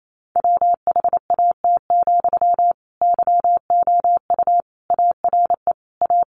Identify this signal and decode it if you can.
Morse code, heard as a keyed tone, about 28 wpm, 700 Hz. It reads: WHAT, YOU ARE A
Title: Sentences from Top 1300 Words 28wpm